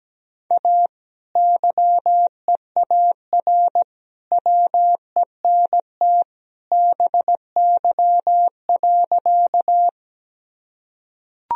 Morse code, heard as a keyed tone, about 17 wpm, 700 Hz. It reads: A YEAR WENT BY.
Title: Sentences from Top 1000 Words - Review Part 1 - 17wpm